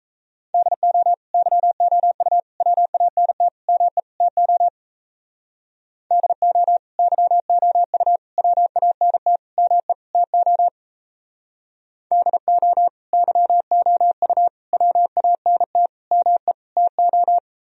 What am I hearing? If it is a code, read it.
Morse code, heard as a keyed tone, about 34 wpm, 700 Hz. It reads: DO YOU WANT ME TO DO YOU WANT ME TO DO YOU WANT ME TO
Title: Speed-Racing - Sets of 5 Words 28wpm